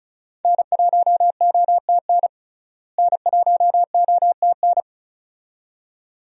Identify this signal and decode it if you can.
Morse code, heard as a keyed tone, about 34 wpm, 700 Hz. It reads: N1OTN N1OTN
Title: USA Call Signs - 5 Characters - Repeated 2x - 35wpm